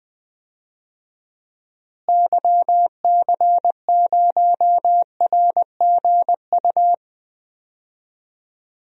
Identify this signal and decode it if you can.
Morse code, heard as a keyed tone, about 20 wpm, 700 Hz. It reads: YC0RGU
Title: DXCC-100 Call Signs - 6 Characters - Repeated 2x - 20wpm